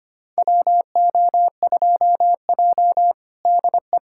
Morse code, heard as a keyed tone, about 25 wpm, 700 Hz. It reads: WO2J DE
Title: USA Callsign DE Callsign - 4 Characters - 25wpm